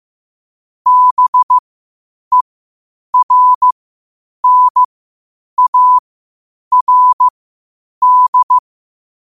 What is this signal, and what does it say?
Morse code, heard as a keyed tone, about 15 wpm, spaced as for 8 wpm, 1000 Hz. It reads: BERNARD
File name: qotdfiles / 8wpm